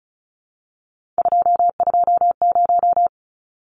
Morse code, heard as a keyed tone, about 34 wpm, 700 Hz. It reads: 220
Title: Numbers - 3-Digit 35wpm